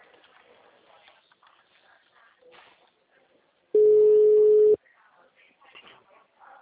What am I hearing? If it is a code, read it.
Ringback tone (Europe, 425 Hz).